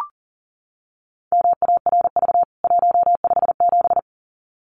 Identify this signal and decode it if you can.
Morse code, heard as a keyed tone, about 40 wpm, 700 Hz. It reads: MARV 157
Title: CWT Exchange 40wpm